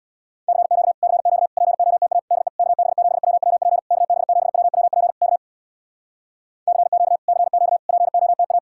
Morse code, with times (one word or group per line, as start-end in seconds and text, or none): 0.5-0.9 CQ
1.0-1.5 CQ
1.6-2.2 POTA
2.3-2.5 DE
2.6-3.8 KB7WWP
3.9-5.1 KB7WWP
5.2-5.4 K
6.7-7.2 CQ
7.3-7.8 CQ
7.9-8.6 POTA